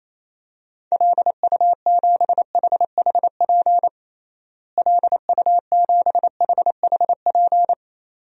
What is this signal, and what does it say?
Morse code, heard as a keyed tone, about 28 wpm, 700 Hz. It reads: LU7HHP LU7HHP
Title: DXCC-100 Call Signs - 6 Characters - Repeated 2x - 28wpm